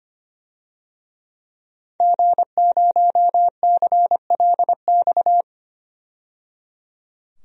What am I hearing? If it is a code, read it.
Morse code, heard as a keyed tone, about 25 wpm, 700 Hz. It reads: G0CLX
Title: DXCC-100 Call Signs - 5 Characters - 25wpm